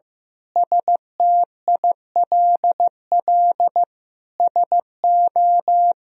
Morse code, heard as a keyed tone, about 15 wpm, 700 Hz. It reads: STILL SO
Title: Sentences from Top 400 Words and Abbreviations 15wpm